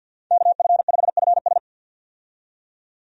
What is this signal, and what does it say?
Morse code, heard as a keyed tone, about 50 wpm, 700 Hz. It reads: KF5FS
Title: USA Call Signs - 5 Characters - 50wpm